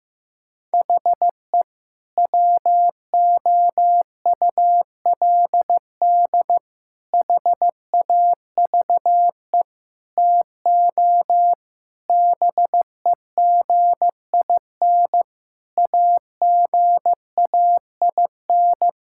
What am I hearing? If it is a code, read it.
Morse code, heard as a keyed tone, about 16 wpm, 700 Hz. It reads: HE WOULD HAVE TO BEGIN AGAIN
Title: Sentences from Top 1000 Words - Review Part 5 - 15wpm